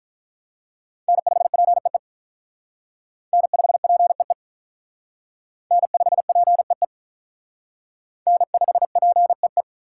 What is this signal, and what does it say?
Morse code, heard as a keyed tone, about 45 wpm, 700 Hz. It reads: N5PEE N5PEE N5PEE N5PEE